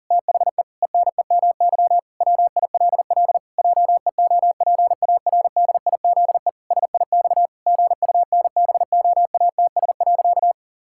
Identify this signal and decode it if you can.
Morse code, heard as a keyed tone, about 37 wpm, 700 Hz. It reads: THE ENEMY WILL JEOPARDIZE SIX GUNBOATS.
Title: Pangram Challenge v2 - 40wpm